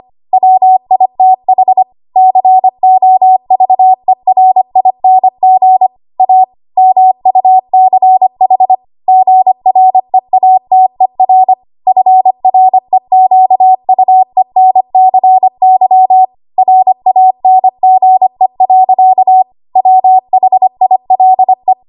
Morse code, heard as a keyed tone, about 25 wpm, 750 Hz. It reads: WITH COVERING A MUCH GREATER FREQUENCY RANGE. WHILE
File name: Archive / 25 WPM / 25WPM